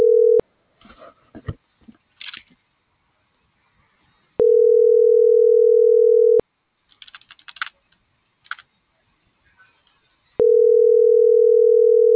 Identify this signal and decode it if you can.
Ringback tone (North America), 440 + 480 Hz.